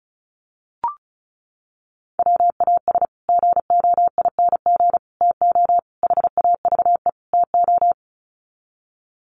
Morse code, heard as a keyed tone, about 34 wpm, 700 Hz. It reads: WAS GOING TO HAVE TO
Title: Sets of 5 Words Encore 35wpm